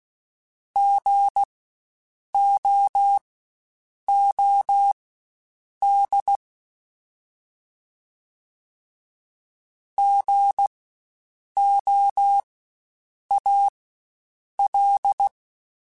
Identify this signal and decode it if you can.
Morse code, heard as a keyed tone, about 16 wpm, spaced as for 7 wpm, 800 Hz. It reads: GOOD GOAL
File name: Spaced Alphas / Spaced Alphas mp3s / Gs sp @15